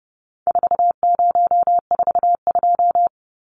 Morse code, heard as a keyed tone, about 30 wpm, 700 Hz. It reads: 4042
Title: Numbers - 4-Digit 30wpm